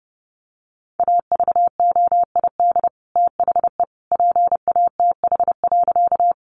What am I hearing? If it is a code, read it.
Morse code, heard as a keyed tone, about 30 wpm, 700 Hz. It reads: AVOID THE PATH.
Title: Sentences from Top 1400 Words 30wpm